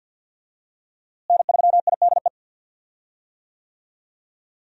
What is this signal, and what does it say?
Morse code, heard as a keyed tone, about 52 wpm, 700 Hz. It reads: N3IDE
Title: USA Call Signs - 5 Characters - Repeated 2x - 50wpm